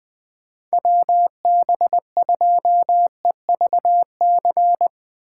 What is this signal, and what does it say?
Morse code, heard as a keyed tone, about 20 wpm, 700 Hz. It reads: WB2EVC
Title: Speed Racing - USA 2x3 Call Signs 17wpm